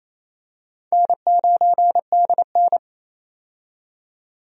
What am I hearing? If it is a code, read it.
Morse code, heard as a keyed tone, about 28 wpm, 700 Hz. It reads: N9DN
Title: USA Call Signs - 4 Characters - 28wpm